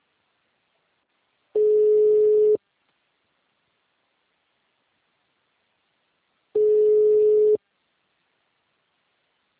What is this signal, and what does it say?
Ringback tone (Europe, 425 Hz).
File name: out- -2028- - - .5418